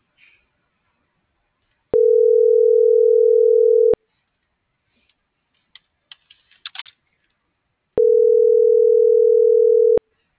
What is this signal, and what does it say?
Ringback tone (North America), 440 + 480 Hz.